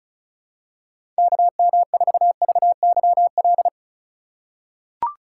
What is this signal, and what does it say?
Morse code, heard as a keyed tone, about 34 wpm, 700 Hz. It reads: KM4VYL
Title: USA Call Signs - 6 Characters - 4x Character Spacing - 35wpm